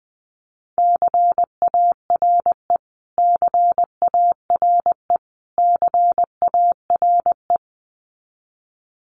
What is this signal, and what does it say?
Morse code, heard as a keyed tone, about 20 wpm, 700 Hz. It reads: CARE CARE CARE
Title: Top 500 Words Repeated 20wpm